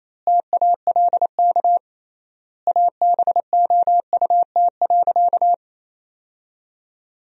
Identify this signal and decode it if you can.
Morse code, heard as a keyed tone, about 28 wpm, 700 Hz. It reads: TALK ABOUT.
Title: 3x Word-Spacing - Sentences from Top 400 Words - 28wpm